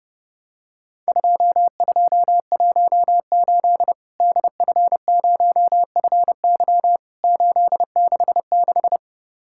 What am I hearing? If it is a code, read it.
Morse code, heard as a keyed tone, about 30 wpm, 700 Hz. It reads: 2218 DF0FY 866
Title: Worked All Europe DX Contest - QTC Exchange - 30wpm